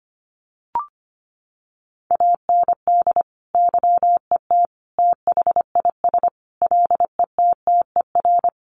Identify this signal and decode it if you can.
Morse code, heard as a keyed tone, about 25 wpm, 700 Hz. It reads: AND YET THIS LETTER
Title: Sentences from Top 1400 Words 25wpm